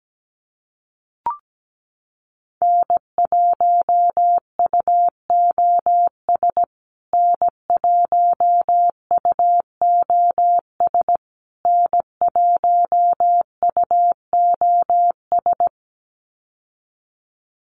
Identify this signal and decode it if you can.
Morse code, heard as a keyed tone, about 17 wpm, 700 Hz. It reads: N1UOS N1UOS N1UOS
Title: USA Call Signs Repeated 3x 17wpm